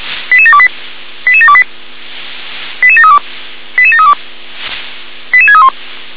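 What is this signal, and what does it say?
Five-tone selective call (ZVEI), heard as quick runs of five tones, tones of about 70 ms. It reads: 88728 88728 88732 88732 88631
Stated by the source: ZVEI five-tone selective call received on SDR, code 88510.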